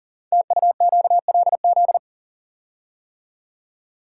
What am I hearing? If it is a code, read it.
Morse code, heard as a keyed tone, about 42 wpm, 690 Hz. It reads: TUQLZ
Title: Code Groups - 5 Letters - 40wpm